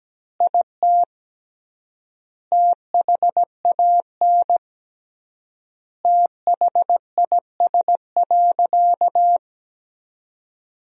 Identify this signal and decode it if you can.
Morse code, heard as a keyed tone, about 17 wpm, 700 Hz. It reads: IT THAN THIS.
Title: 3x Word-Spacing - Sentences from Top 100 Words - 17wpm